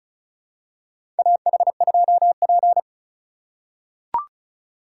Morse code, heard as a keyed tone, about 34 wpm, 700 Hz. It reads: AH2P
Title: USA Call Signs - 3x Character Spacing - 35wpm